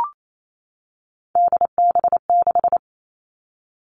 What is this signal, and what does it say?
Morse code, heard as a keyed tone, about 28 wpm, 700 Hz. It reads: DB6
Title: Three Challenging Letter-Number Combinations 28wpm